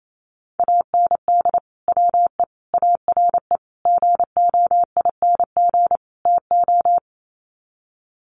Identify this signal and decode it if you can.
Morse code, heard as a keyed tone, about 28 wpm, 700 Hz. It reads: AND WE ARE GOING TO
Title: Sets of 5 Words Encore2 28wpm